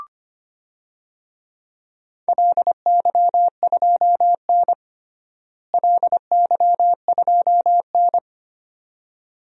Morse code, heard as a keyed tone, about 25 wpm, 700 Hz. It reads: LY2N LY2N
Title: DXCC-100 Call Signs - 4 Characters - Repeated 2x - 25wpm